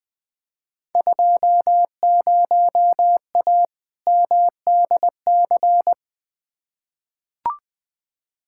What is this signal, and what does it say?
Morse code, heard as a keyed tone, about 21 wpm, 700 Hz. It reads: 20A MDC
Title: Field Day Exchanges Encore - 20wpm